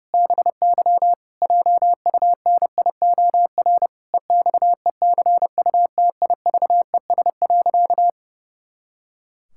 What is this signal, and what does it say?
Morse code, heard as a keyed tone, about 30 wpm, 700 Hz. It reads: BY JUNIOR EXECUTIVES.
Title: Pangram Challenge v2 - 30wpm